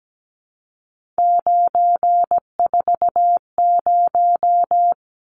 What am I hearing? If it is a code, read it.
Morse code, heard as a keyed tone, about 17 wpm, 700 Hz. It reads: 940
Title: Numbers - 3-Digit 17wpm